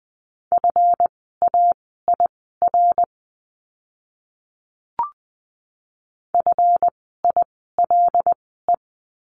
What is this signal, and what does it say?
Morse code, heard as a keyed tone, about 20 wpm, spaced as for 15 wpm, 700 Hz. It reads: FAIR FILE
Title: 139 Lesson 23f - Letter f - Words 20wpm